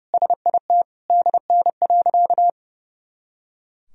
Morse code, heard as a keyed tone, about 30 wpm, 700 Hz. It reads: SIT DN.
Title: Sentences from Top 500 Words and Abbreviations 30wpm